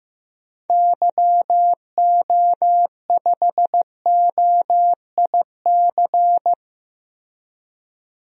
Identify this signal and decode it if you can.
Morse code, heard as a keyed tone, about 15 wpm, 700 Hz. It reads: YO5OIC
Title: DXCC-100 Call Signs - 6 Characters - Repeated 2x - 15wpm